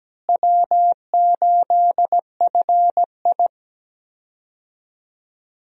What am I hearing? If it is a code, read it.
Morse code, heard as a keyed tone, about 17 wpm, 700 Hz. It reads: W8FI
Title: USA Call Signs - 4 Characters - Repeated 3x - 17wpm